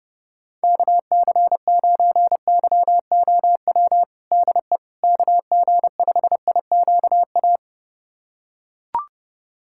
Morse code, heard as a keyed tone, about 30 wpm, 700 Hz. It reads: KC9YOW DE KG5IQA
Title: USA Callsign DE Callsign - 6 Characters - 30wpm